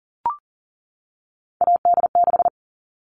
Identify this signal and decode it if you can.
Morse code, heard as a keyed tone, about 40 wpm, 700 Hz. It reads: AD6
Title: All Cut Numbers - 3-Digit 40wpm